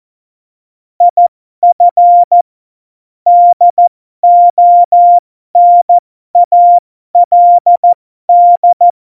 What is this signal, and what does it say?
Morse code, heard as a keyed tone, about 14 wpm, spaced as for 12 wpm, 700 Hz. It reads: IF DONALD
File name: Morsekurs / Witze